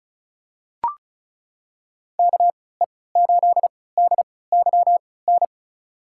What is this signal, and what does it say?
Morse code, heard as a keyed tone, about 36 wpm, spaced as for 20 wpm, 700 Hz. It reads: KE8DYN
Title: USA Call Signs - 6 Characters - 3x Character Spacing - 35wpm